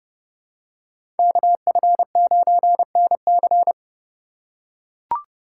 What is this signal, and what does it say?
Morse code, heard as a keyed tone, about 30 wpm, 700 Hz. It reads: KF9NC
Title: USA Call Signs - 5 Characters - 8x Character Spacing - 30wpm